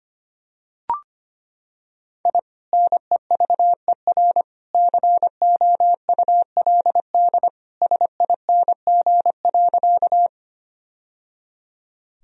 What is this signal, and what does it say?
Morse code, heard as a keyed tone, about 25 wpm, 700 Hz. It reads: I NEVER COULD SING.
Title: Sentences from Top 1000 Words - Review Part 2 - 25wpm